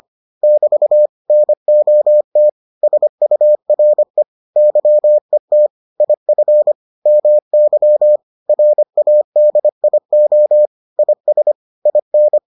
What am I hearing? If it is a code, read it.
Morse code, heard as a keyed tone, about 25 wpm, 600 Hz. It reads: <BT> NOT SURE YET IF MY RADIO IS IN